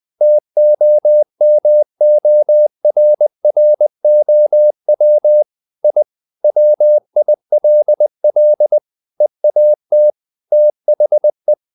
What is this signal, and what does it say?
Morse code, heard as a keyed tone, about 20 wpm, 600 Hz. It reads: TOMORROW I WILL EAT THE
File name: the old man and the sea part 2 20wpm